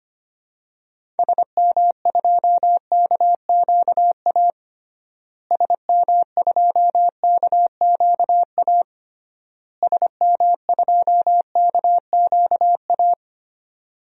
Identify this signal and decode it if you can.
Morse code, heard as a keyed tone, about 25 wpm, 700 Hz. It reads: SM2KQA SM2KQA SM2KQA
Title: DXCC-100 Call Signs - 6 Characters - Repeated 3x - 25wpm